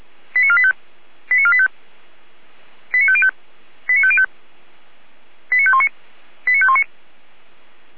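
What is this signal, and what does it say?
Five-tone selective call (ZVEI), heard as quick runs of five tones, tones of about 70 ms. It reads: 89485 89485 89505 89505 89519 89519
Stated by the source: ZVEI five-tone call received by SDR.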